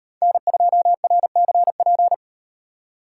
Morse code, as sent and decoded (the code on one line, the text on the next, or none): -. ..--- .-. -.-. .--.
N2RCP